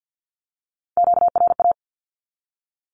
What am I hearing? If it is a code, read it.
Morse code, heard as a keyed tone, about 50 wpm, 700 Hz. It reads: QRA
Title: Q-Codes 50wpm